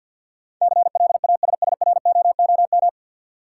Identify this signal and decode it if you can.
Morse code, heard as a keyed tone, about 46 wpm, 700 Hz. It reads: CLASSROOM
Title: Sentences from Top 1500 Words - New Word Drill - 50wpm